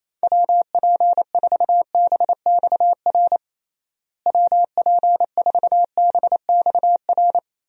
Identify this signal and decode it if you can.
Morse code, heard as a keyed tone, about 28 wpm, 700 Hz. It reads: WP4BXR WP4BXR
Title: USA Call Signs - 6 Characters - Repeated 2x - 28wpm